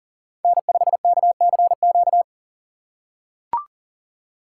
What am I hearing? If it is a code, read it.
Morse code, heard as a keyed tone, about 40 wpm, 700 Hz. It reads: N5KCQ
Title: USA Call Signs - 5 Characters - 4x Character Spacing - 40wpm